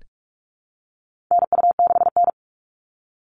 Morse code, heard as a keyed tone, about 44 wpm, 700 Hz. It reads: NU6N
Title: Call Signs Repeated 3x - 45wpm